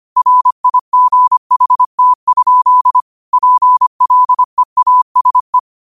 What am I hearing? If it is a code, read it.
Morse code, heard as a keyed tone, about 25 wpm, 1000 Hz. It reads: RIGHT? PLEASE
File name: qotdfiles / 25wpm